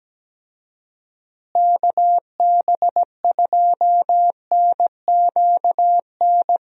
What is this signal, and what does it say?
Morse code, heard as a keyed tone, about 17 wpm, 700 Hz. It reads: KB2NQN
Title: USA Call Signs - 6 Characters - Repeated 2x - 17wpm